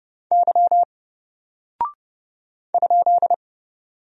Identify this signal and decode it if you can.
Morse code, heard as a keyed tone, about 30 wpm, 700 Hz. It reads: Y ?
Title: Single Character (Letter:Number:Punctuation:Prosign) 30wpm